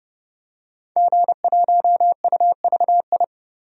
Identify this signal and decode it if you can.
Morse code, heard as a keyed tone, about 30 wpm, 700 Hz. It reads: G1UVI
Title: DXCC-100 Call Signs - 5 Characters - 30wpm